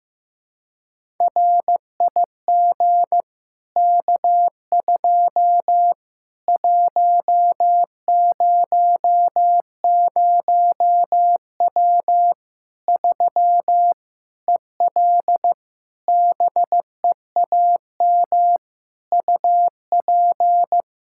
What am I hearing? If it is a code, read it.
Morse code, heard as a keyed tone, about 15 wpm, 700 Hz. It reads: RIG K2 100W 3 EL BEAM UP
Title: K1USN SST QSO Practice 15wpm